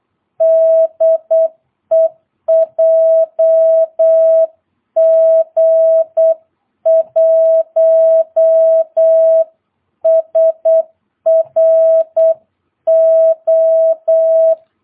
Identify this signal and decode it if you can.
Morse code, heard as a keyed tone, about 8 wpm, 660 Hz. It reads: DE JG1SRO
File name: jh8cbh / 2022cw